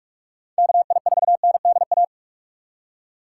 Morse code, heard as a keyed tone, about 43 wpm, 700 Hz. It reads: KI4NDA
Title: USA Call Signs - 6 Characters - 3x Character Spacing - 45wpm